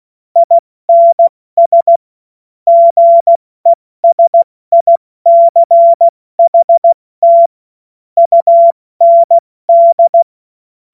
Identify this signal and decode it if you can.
Morse code, heard as a keyed tone, about 16 wpm, 680 Hz. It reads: INS GESICHT UND
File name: Morsekurs / Witze / Witz 07. Februar 2023 70 F80